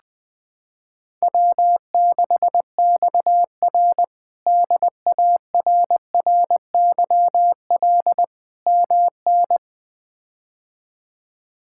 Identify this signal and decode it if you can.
Morse code, heard as a keyed tone, about 19 wpm, 700 Hz. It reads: W6XR DARRYL MN
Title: K1USN SST QSO Practice - 20wpm